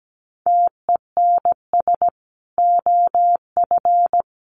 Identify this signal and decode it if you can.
Morse code, heard as a keyed tone, about 17 wpm, 700 Hz. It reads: TENS OF
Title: Sets of 5 Words Encore 17wpm